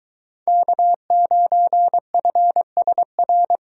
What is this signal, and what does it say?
Morse code, heard as a keyed tone, about 22 wpm, 700 Hz. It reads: K9FSR